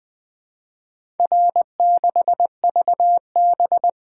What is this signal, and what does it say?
Morse code, heard as a keyed tone, about 20 wpm, 700 Hz. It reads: R6VB
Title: Four Letter-Number Combinations - 20wpm